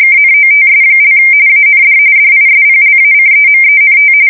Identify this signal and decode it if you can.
RTTY, about 45.45 baud. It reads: KNOTS) AND DISPLACEMENT
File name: fsk170 / USS-New-Jersey